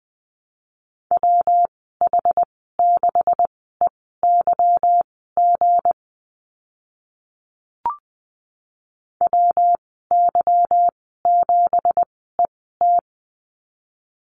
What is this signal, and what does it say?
Morse code, heard as a keyed tone, about 20 wpm, spaced as for 14 wpm, 700 Hz. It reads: WH6EYG WY7ET
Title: 167 Lesson 27g - Number 7 - Call Signs 20wpm